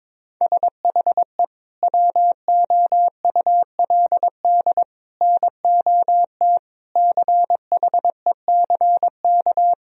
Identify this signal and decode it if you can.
Morse code, heard as a keyed tone, about 22 wpm, 700 Hz. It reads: SHE WOULD NOT CHECK
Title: Sentences from Top 1000 Words - Review Part 3 - 22wpm